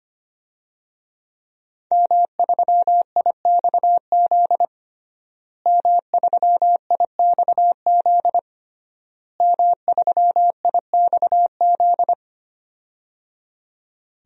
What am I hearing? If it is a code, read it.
Morse code, heard as a keyed tone, about 25 wpm, 700 Hz. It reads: M3IXZ M3IXZ M3IXZ
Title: DXCC-100 Call Signs - 5 Characters - Repeated 3x - 25wpm